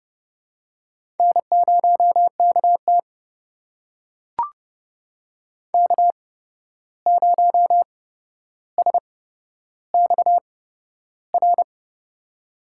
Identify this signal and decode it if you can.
Morse code, heard as a keyed tone, about 31 wpm, spaced as for 12 wpm, 700 Hz. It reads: N0KT K0SXR
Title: USA Call Signs - 8x Character Spacing - 30wpm